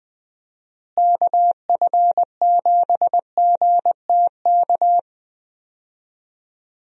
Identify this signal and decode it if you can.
Morse code, heard as a keyed tone, about 20 wpm, 700 Hz. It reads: KF7GTK
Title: USA Call Signs - 6 Characters - Repeated 2x - 20wpm